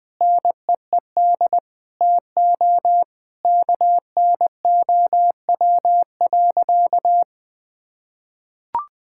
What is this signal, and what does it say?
Morse code, heard as a keyed tone, about 20 wpm, 700 Hz. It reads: NEED TO KNOW.
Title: Sentences from Top 1000 Words - Review Part 5 - 20wpm